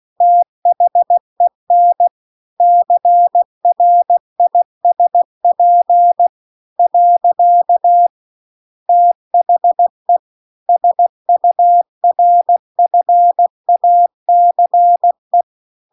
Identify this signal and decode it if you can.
Morse code, heard as a keyed tone, about 16 wpm, 700 Hz. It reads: THEN CRISP . THE SURFACE
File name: CW-The-Big-Twoheaded-River-2-16WPM